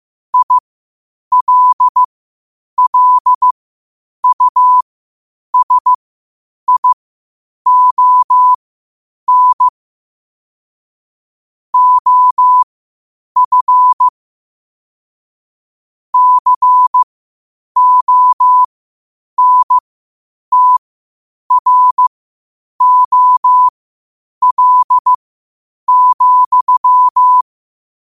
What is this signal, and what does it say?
Morse code, heard as a keyed tone, about 15 wpm, spaced as for 8 wpm, 1000 Hz. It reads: ILLUSION OF CONTROL,
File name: qotdfiles / 8wpm